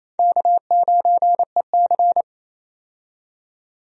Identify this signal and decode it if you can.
Morse code, heard as a keyed tone, about 28 wpm, 700 Hz. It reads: K9EC
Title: USA Call Signs - 4 Characters - 28wpm